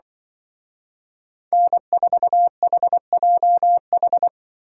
Morse code, heard as a keyed tone, about 24 wpm, 700 Hz. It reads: N4HJH
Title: Speed Racing - USA 1x3 Call Signs 20wpm